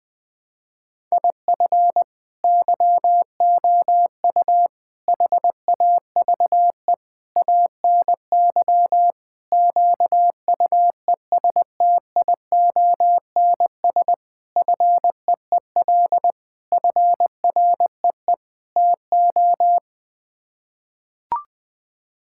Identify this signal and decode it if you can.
Morse code, heard as a keyed tone, about 20 wpm, 700 Hz. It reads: IF YOU HAVE ANY QUESTIONS FEEL FREE TO
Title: Sets of 8 Words - 20wpm